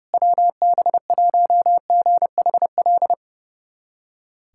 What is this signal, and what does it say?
Morse code, heard as a keyed tone, about 32 wpm, 700 Hz. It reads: WB1GHL
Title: USA Callsign DE Callsign - 6 Characters - 30wpm